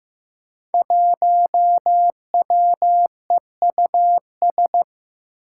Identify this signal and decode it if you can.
Morse code, heard as a keyed tone, about 16 wpm, 700 Hz. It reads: 1WEUS
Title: Five Challenging Letter-Number Combinations 15wpm